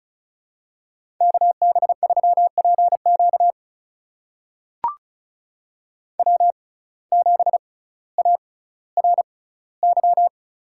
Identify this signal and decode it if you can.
Morse code, heard as a keyed tone, about 36 wpm, spaced as for 18 wpm, 700 Hz. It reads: KB3PQ W7ARY
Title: USA Call Signs - 5 Characters - 6x Character Spacing - 35wpm